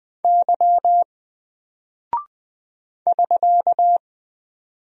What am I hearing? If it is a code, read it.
Morse code, heard as a keyed tone, about 21 wpm, 700 Hz. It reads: Y <SK>
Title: Single Character (Letter:Number:Punctuation:Prosign) Alt 20wpm